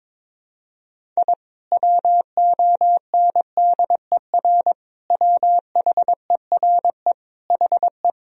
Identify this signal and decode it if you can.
Morse code, heard as a keyed tone, about 22 wpm, 700 Hz. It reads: I WONDER WHERE HE
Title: Sentences from Top 1000 Words - Review Part 3 - 22wpm